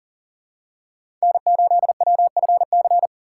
Morse code, heard as a keyed tone, about 40 wpm, 700 Hz. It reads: N8WFC
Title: USA Call Signs - 5 Characters - 2x Character Spacing - 40wpm